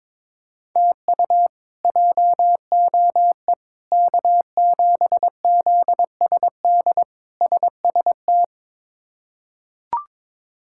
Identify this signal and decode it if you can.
Morse code, heard as a keyed tone, about 22 wpm, 700 Hz. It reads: TU JOE K7ZSD SST
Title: K1USN SST QSO Practice - 22wpm